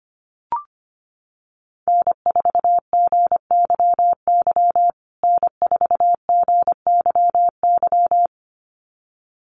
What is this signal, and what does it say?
Morse code, heard as a keyed tone, about 25 wpm, 700 Hz. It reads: N4GYY N4GYY
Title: USA Call Signs Repeated 2x 25wpm